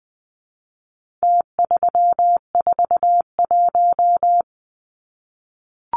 Morse code, heard as a keyed tone, about 20 wpm, 700 Hz. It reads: T341
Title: Numbers - Contest Style - 4-Digit 20wpm